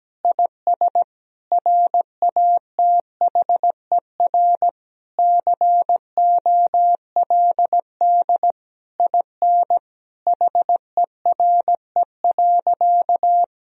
Morse code, heard as a keyed tone, about 17 wpm, 700 Hz. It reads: IS RATHER COLD IN HERE.
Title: Sentences from Top 1000 Words - Review Part 4 - 17wpm